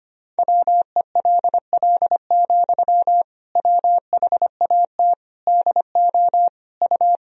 Morse code, heard as a keyed tone, about 25 wpm, 700 Hz. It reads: WELL, WHAT DO U
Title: Sentences from Top 200 Words and Abbreviations 25wpm